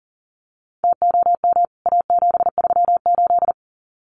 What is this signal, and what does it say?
Morse code, heard as a keyed tone, about 40 wpm, 700 Hz. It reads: TOM A738
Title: CWT Exchange 40wpm